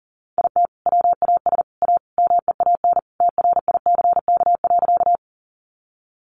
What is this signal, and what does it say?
Morse code, heard as a keyed tone, about 37 wpm, 700 Hz. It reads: IT WAS A MEAN TRICK.
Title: Sentences from Top 1500 Words 40wpm